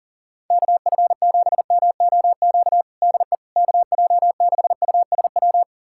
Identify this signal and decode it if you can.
Morse code, heard as a keyed tone, about 40 wpm, 690 Hz. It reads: KF7MOQ DE KJ6USW
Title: USA Callsign DE Callsign - 6 Characters - 40wpm